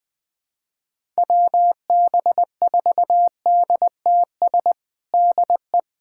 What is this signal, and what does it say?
Morse code, heard as a keyed tone, about 21 wpm, 700 Hz. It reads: WB4DTS DE
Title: USA Callsign DE Callsign - 6 Characters - 20wpm